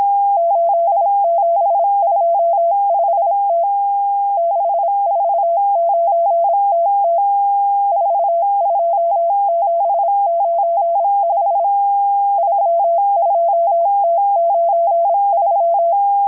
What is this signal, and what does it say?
Morse code, heard as a keyed tone, about 26 wpm, 680 Hz. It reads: 8725T 649TT 42795 32T93
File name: HM02- -0437utc-7351khz